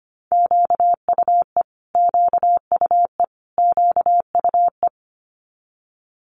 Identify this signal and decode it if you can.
Morse code, heard as a keyed tone, about 25 wpm, 700 Hz. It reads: QUE QUE QUE
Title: Multi-letter Phonemes Repeated 3x 25wpm